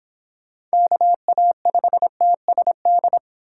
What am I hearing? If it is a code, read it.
Morse code, heard as a keyed tone, about 24 wpm, 700 Hz. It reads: KA5TSD